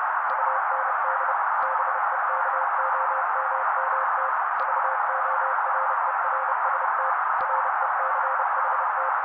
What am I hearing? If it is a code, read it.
Morse code, heard as a keyed tone, about 30 wpm, 530 Hz. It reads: UND BEKOMMT VORERST RECHT
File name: daily / 29wpm noise